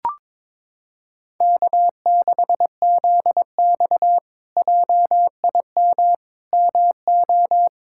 Morse code, heard as a keyed tone, about 22 wpm, 700 Hz. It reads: K6ZX JIM MO
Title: K1USN SST QSO Practice - 22wpm